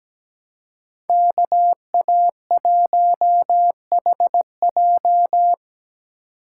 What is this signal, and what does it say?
Morse code, heard as a keyed tone, about 20 wpm, 700 Hz. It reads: KA1HJ
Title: Speed Racing - USA 2x2 Call Signs 17wpm